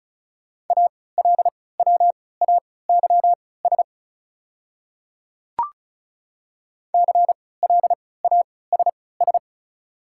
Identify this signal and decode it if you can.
Morse code, heard as a keyed tone, about 34 wpm, spaced as for 20 wpm, 700 Hz. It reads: ALWAYS CLASS
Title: 3x Character Spacing - Top 500 Words - 35wpm